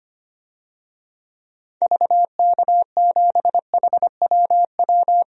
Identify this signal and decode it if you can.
Morse code, heard as a keyed tone, about 25 wpm, 700 Hz. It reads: VK7HWW
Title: DXCC-100 Call Signs - 6 Characters - Repeated 3x - 25wpm